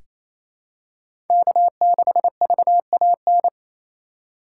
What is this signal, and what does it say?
Morse code, heard as a keyed tone, about 28 wpm, 700 Hz. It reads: K6VAN
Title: USA Call Signs - 5 Characters - 2x Character Spacing - 28wpm